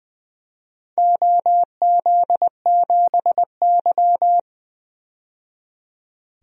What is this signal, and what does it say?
Morse code, heard as a keyed tone, about 20 wpm, 700 Hz. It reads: OZ7Y
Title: DXCC-100 Call Signs - 4 Characters - 20wpm